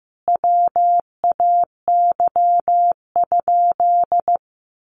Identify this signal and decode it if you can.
Morse code, heard as a keyed tone, about 15 wpm, 700 Hz. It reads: WAY?
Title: Sentences from Top 1200 Words 15wpm